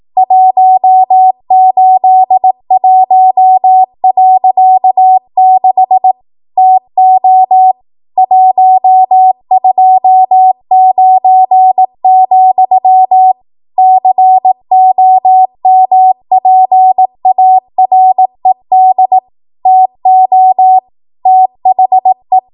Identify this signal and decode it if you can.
Morse code, heard as a keyed tone, about 18 wpm, 750 Hz. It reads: 181.6 TO 129, COMPARED TO THE
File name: Archive / 18 WPM / 18WPM